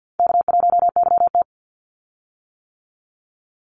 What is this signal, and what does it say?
Morse code, heard as a keyed tone, about 48 wpm, 700 Hz. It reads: K1YT
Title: Easy Call Signs - 50wpm